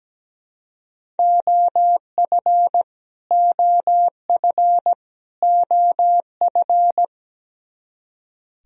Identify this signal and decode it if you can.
Morse code, heard as a keyed tone, about 17 wpm, 700 Hz. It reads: OF OF OF
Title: Top 100 Words Repeated 17wpm